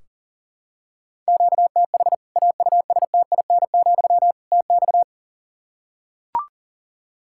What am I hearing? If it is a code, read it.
Morse code, heard as a keyed tone, about 40 wpm, 700 Hz. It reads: QTH AUSTIN, TX
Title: Practice QSO - Exchange 1 - Beginning - 40wpm